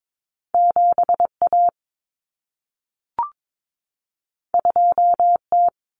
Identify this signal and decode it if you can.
Morse code, heard as a keyed tone, about 22 wpm, 700 Hz. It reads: 7A 2T
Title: Common Cut Numbers - 2-Digit 22wpm